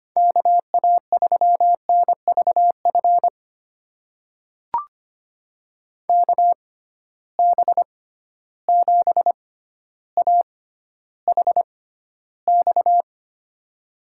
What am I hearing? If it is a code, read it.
Morse code, heard as a keyed tone, about 25 wpm, spaced as for 13 wpm, 700 Hz. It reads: KA3NVF KB7AHX
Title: USA Call Signs - 6 Characters - 6x Character Spacing - 25wpm